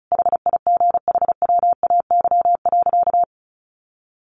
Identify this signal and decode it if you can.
Morse code, heard as a keyed tone, about 34 wpm, 700 Hz. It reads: HIGHWAY.
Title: Sentences from Top 1400 Words 35wpm